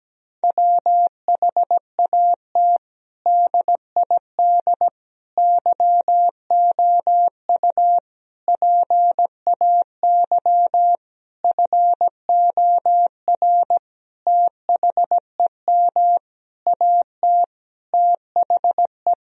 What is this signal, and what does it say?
Morse code, heard as a keyed tone, about 17 wpm, 700 Hz. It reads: WHAT DID YOU PAY FOR THEM AT THE
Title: Sentences from Top 1000 Words - Review Part 3 - 17wpm